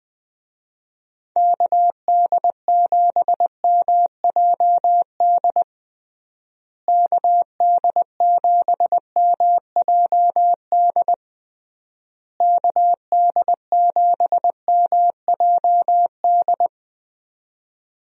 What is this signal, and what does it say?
Morse code, heard as a keyed tone, about 20 wpm, 700 Hz. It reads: KD7MJD KD7MJD KD7MJD
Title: USA Call Signs - 6 Characters - Repeated 3x - 20wpm